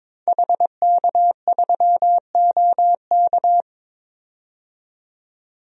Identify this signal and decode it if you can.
Morse code, heard as a keyed tone, about 21 wpm, 700 Hz. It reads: HK3OK
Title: DXCC-100 Call Signs - 5 Characters - 22wpm